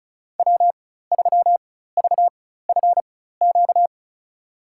Morse code, heard as a keyed tone, about 36 wpm, spaced as for 16 wpm, 700 Hz. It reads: W3VFQ
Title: USA Call Signs - 5 Characters - 4x Character Spacing - 35wpm